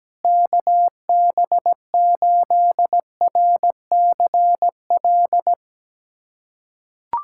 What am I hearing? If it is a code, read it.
Morse code, heard as a keyed tone, about 17 wpm, 700 Hz. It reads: KB8RCL
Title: USA Callsign DE Callsign 17wpm